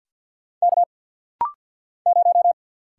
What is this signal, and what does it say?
Morse code, heard as a keyed tone, about 48 wpm, 700 Hz. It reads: K 0
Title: Single Letter-Number - Rapid-Fire 50wpm